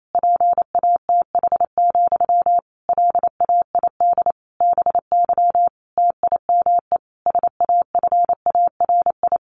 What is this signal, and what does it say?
Morse code, heard as a keyed tone, about 28 wpm, 700 Hz. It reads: PATH, LAID BY TIME SAFARI
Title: A Sound Of Thunder 28wpm